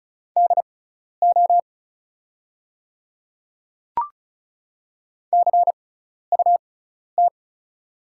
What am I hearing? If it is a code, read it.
Morse code, heard as a keyed tone, about 36 wpm, spaced as for 12 wpm, 690 Hz. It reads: DO CUT
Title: 6x Character Spacing - Top 300 Words - 35wpm